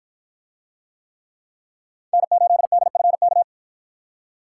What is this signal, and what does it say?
Morse code, heard as a keyed tone, about 53 wpm, 700 Hz. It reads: N8DFK